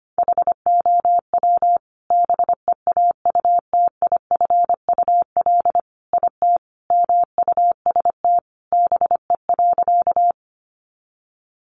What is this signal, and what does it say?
Morse code, heard as a keyed tone, about 25 wpm, 700 Hz. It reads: HOW BEAUTIFUL IT MUST BE.
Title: Sentences from Top 900 Words 25wpm